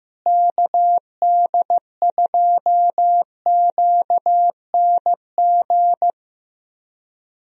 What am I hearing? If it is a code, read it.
Morse code, heard as a keyed tone, about 15 wpm, 700 Hz. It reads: KD2QNG
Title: USA Call Signs - 6 Characters - Repeated 3x - 15wpm